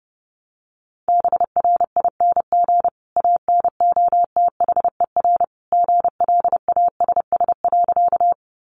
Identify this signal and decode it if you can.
Morse code, heard as a keyed tone, about 30 wpm, 700 Hz. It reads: BRING ANOTHER GLASS.
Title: Sentences from Top 1300 Words 30wpm